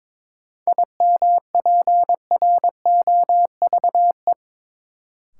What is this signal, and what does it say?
Morse code, heard as a keyed tone, about 21 wpm, 700 Hz. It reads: IMPROVE
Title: Sentences from Top 800 Words - New Word Drill - 22wpm